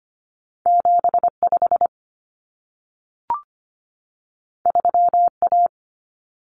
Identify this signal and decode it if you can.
Morse code, heard as a keyed tone, about 25 wpm, 700 Hz. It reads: 75 3A
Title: Common Cut Numbers - 2-Digit 25wpm